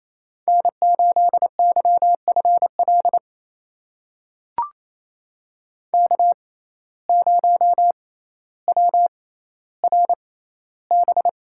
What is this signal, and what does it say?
Morse code, heard as a keyed tone, about 28 wpm, spaced as for 14 wpm, 700 Hz. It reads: N8YFL K0WRB
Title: USA Call Signs - 5 Characters - 6x Character Spacing - 28wpm